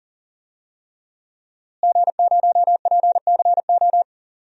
Morse code, heard as a keyed tone, about 40 wpm, 700 Hz. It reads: G0PCO
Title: DXCC-100 Call Signs - 5 Characters - Repeated 3x - 40wpm